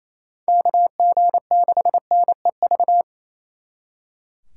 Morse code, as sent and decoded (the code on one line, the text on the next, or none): -.- --. -.... -. ...-
KG6NEV